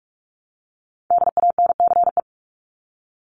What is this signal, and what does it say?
Morse code, heard as a keyed tone, about 44 wpm, 700 Hz. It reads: DANCE
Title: Top 501-750 Words 45wpm